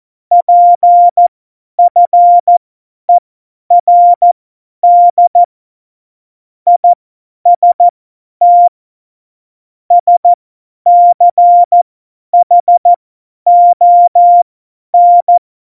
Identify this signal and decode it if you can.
Morse code, heard as a keyed tone, about 14 wpm, spaced as for 10 wpm, 700 Hz. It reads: PFERD IST SCHON
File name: Morsekurs / Witze / Witz 26 Mai 2020, 50 F70